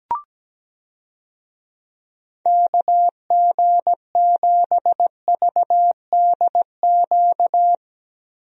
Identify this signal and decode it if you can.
Morse code, heard as a keyed tone, about 17 wpm, 700 Hz. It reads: KG7VDQ
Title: USA Call Signs - 6 Characters - Repeated 2x - 17wpm